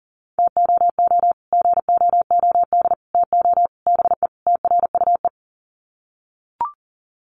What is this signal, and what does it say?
Morse code, heard as a keyed tone, about 40 wpm, 700 Hz. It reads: TOO GOOD TO BE TRUE
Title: Sets of 5 Words Encore 40wpm